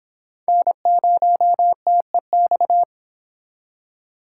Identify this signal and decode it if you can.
Morse code, heard as a keyed tone, about 26 wpm, 700 Hz. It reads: N0TEX